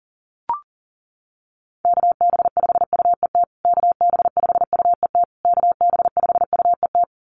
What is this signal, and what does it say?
Morse code, heard as a keyed tone, about 40 wpm, 700 Hz. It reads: KB5UET KB5UET KB5UET
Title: USA Call Signs Repeated 3x 40wpm